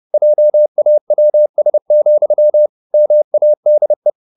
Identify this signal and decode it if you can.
Morse code, heard as a keyed tone, about 30 wpm, 600 Hz. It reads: JAWS, MADE